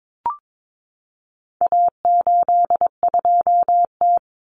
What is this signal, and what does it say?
Morse code, heard as a keyed tone, about 22 wpm, 700 Hz. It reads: A82T
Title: Common Cut Numbers - 4-Digit 22wpm